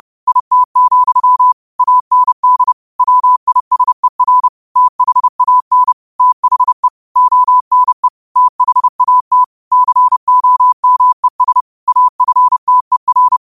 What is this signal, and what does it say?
Morse code, heard as a keyed tone, about 30 wpm, 1000 Hz. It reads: IT, AND WISER THAN THE ONE THAT COMES AFTER
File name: qotdfiles / 30wpm